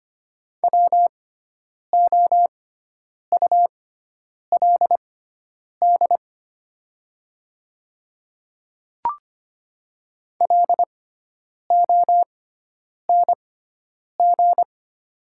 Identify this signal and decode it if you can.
Morse code, heard as a keyed tone, about 25 wpm, spaced as for 9 wpm, 700 Hz. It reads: WOULD LONG
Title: 6x Character Spacing - Top 100 Words - 25wpm